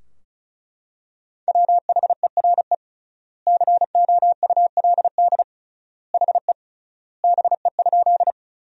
Morse code, heard as a keyed tone, about 34 wpm, 700 Hz. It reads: WHERE COULD HE BE?
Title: 3x Word-Spacing - Sentences from Top 200 Words - 35wpm